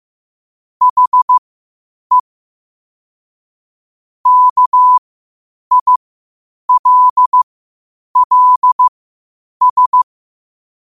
Morse code, heard as a keyed tone, about 15 wpm, spaced as for 8 wpm, 1000 Hz. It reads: HE KILLS
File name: qotdfiles / 8wpm